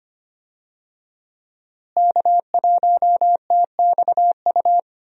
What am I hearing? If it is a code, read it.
Morse code, heard as a keyed tone, about 25 wpm, 700 Hz. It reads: K1TXU
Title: USA Call Signs - 5 Characters - Repeated 2x - 25wpm